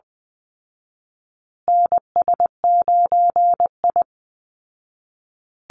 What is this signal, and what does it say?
Morse code, heard as a keyed tone, about 20 wpm, 700 Hz. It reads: NS9I
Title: Call Signs Repeated 2x - 20wpm